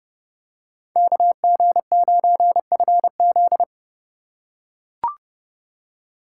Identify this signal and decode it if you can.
Morse code, heard as a keyed tone, about 28 wpm, 700 Hz. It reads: KG9FZ